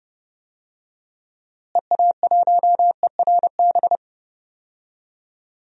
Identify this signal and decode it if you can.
Morse code, heard as a keyed tone, about 30 wpm, 700 Hz. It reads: EA1ERB
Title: DXCC-100 Call Signs - 6 Characters - Repeated 3x - 30wpm